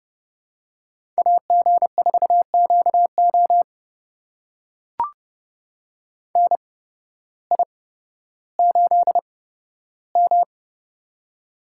Morse code, heard as a keyed tone, about 30 wpm, spaced as for 14 wpm, 700 Hz. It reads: AG4QO NI8M
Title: USA Call Signs - 8x Character Spacing - 30wpm